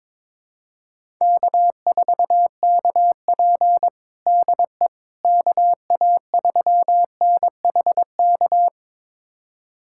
Morse code, heard as a keyed tone, about 22 wpm, 700 Hz. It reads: K4KP DE KA3NHK
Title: USA Callsign DE Callsign 22wpm